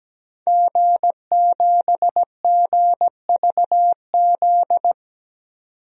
Morse code, heard as a keyed tone, about 17 wpm, 700 Hz. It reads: G7GVZ
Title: DXCC-100 Call Signs - 5 Characters - 17wpm